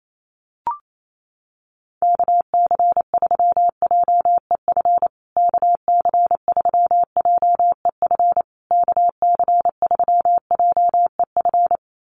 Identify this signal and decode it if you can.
Morse code, heard as a keyed tone, about 28 wpm, 700 Hz. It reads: KC3JEF KC3JEF KC3JEF
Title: USA Call Signs Repeated 3x 28wpm